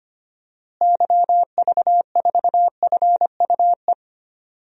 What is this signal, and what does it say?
Morse code, heard as a keyed tone, about 25 wpm, 700 Hz. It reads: YV4FUE
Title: DXCC-100 Call Signs - 6 Characters - Repeated 2x - 25wpm